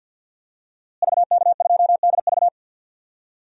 Morse code, heard as a keyed tone, about 50 wpm, 700 Hz. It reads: VK2DV
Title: DXCC-100 Call Signs - 5 Characters - Repeated 2x - 50wpm